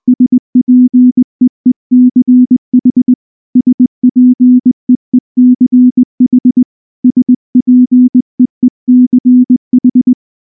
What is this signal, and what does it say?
Morse code, heard as a keyed tone, about 19 wpm, 250 Hz. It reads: SPEECH SPEECH SPEECH
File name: morse / morse-training-20WPM-3